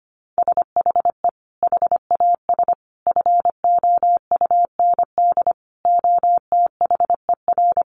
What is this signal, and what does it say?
Morse code, heard as a keyed tone, about 25 wpm, 700 Hz. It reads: SHE HAS FOUND OTHER
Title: Sentences from Top 1500 Words 25wpm